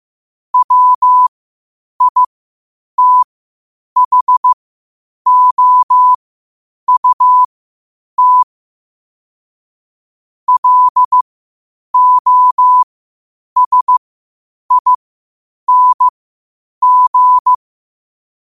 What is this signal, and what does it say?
Morse code, heard as a keyed tone, about 15 wpm, spaced as for 8 wpm, 1000 Hz. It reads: WITHOUT LOSING